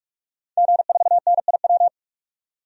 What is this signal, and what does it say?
Morse code, heard as a keyed tone, about 44 wpm, 700 Hz. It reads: G4NIW
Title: DXCC-100 Call Signs - 5 Characters - 45wpm